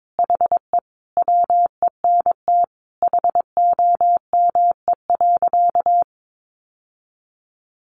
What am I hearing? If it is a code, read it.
Morse code, heard as a keyed tone, about 22 wpm, 700 Hz. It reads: HE WENT HOME.
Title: Sentences from Top 1200 Words 22wpm